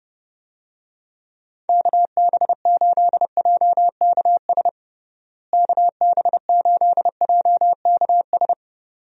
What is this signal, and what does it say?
Morse code, heard as a keyed tone, about 30 wpm, 700 Hz. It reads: KB8JKS KB8JKS
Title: USA Call Signs - 6 Characters - Repeated 2x - 30wpm